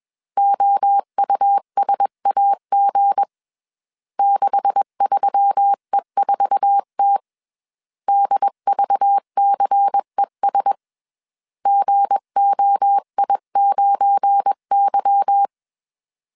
Morse code, heard as a keyed tone, about 21 wpm, 800 Hz. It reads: OUSAG 63E4T DVCES GOI9Y